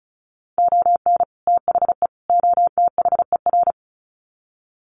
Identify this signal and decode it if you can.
Morse code, heard as a keyed tone, about 34 wpm, 700 Hz. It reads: ON THE OTHER
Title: Sets of 3 Words 35wpm